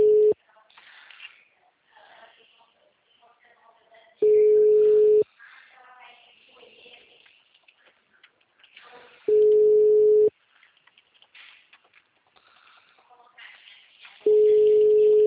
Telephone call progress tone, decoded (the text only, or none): ringback tone